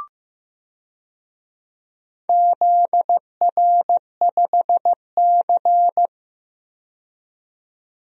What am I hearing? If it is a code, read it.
Morse code, heard as a keyed tone, about 15 wpm, 700 Hz. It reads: ZR5C
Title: DXCC-100 Call Signs - 4 Characters - 15wpm